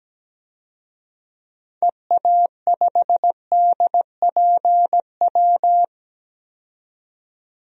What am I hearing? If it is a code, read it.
Morse code, heard as a keyed tone, about 17 wpm, 700 Hz. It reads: EA5DPW
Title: DXCC-100 Call Signs - 6 Characters - 17wpm